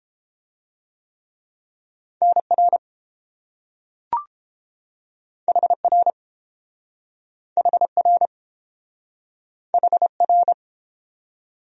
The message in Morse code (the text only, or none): NR HR HR HR